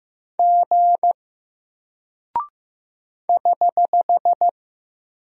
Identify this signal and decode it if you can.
Morse code, heard as a keyed tone, about 15 wpm, 700 Hz. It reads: G <HH>
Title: Single Character (Letter:Number:Punctuation:Prosign) Alt 15wpm